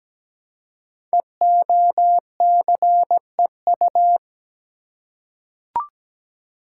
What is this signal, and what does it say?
Morse code, heard as a keyed tone, about 17 wpm, 700 Hz. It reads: EOCEU
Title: Code Groups - 5 Letter:Numbers - 17wpm